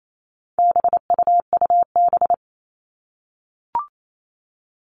Morse code, heard as a keyed tone, about 28 wpm, 700 Hz. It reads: BUUB
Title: All Cut Numbers - 4-Digit 28wpm